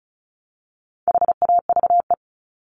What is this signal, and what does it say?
Morse code, heard as a keyed tone, about 34 wpm, 700 Hz. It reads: HAVE
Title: Top 200 Words Repeated 35wpm